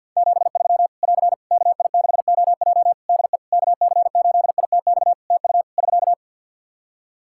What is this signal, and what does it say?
Morse code, heard as a keyed tone, about 51 wpm, 700 Hz. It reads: 73 <AR> KI6OJ DE KC8ITX TU <SK>
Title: K1USN SST QSO Practice 50wpm